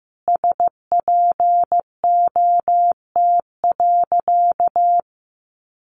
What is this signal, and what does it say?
Morse code, heard as a keyed tone, about 15 wpm, 700 Hz. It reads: SPOT.
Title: Sentences from Top 1200 Words 15wpm